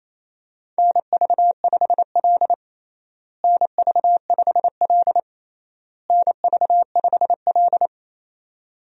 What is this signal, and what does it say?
Morse code, heard as a keyed tone, about 28 wpm, 700 Hz. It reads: NV5L NV5L NV5L
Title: USA Call Signs - 4 Characters - Repeated 3x - 28wpm